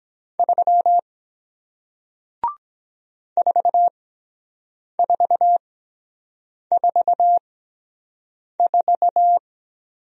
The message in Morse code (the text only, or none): 3 4444